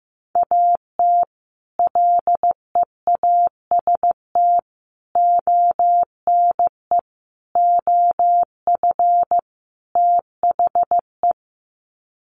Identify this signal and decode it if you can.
Morse code, heard as a keyed tone, about 15 wpm, 700 Hz. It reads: AT LEAST ONE OF THE
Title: Sets of 5 Words Encore 15wpm